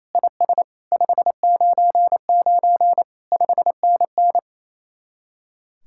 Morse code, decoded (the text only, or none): IS 599 5NN